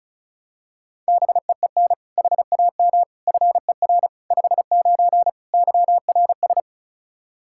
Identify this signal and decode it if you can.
Morse code, heard as a keyed tone, about 34 wpm, 700 Hz. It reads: BEEN HAM FER 59 YRS
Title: Practice QSO - Exchange 3 - More Info - 35wpm